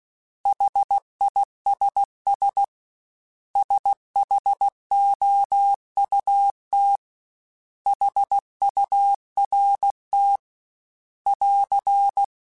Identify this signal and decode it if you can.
Morse code, heard as a keyed tone, about 16 wpm, 800 Hz. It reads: HISS SHOUT HURT <AR>
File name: S vs H / S vs H mp3s / S vs H @15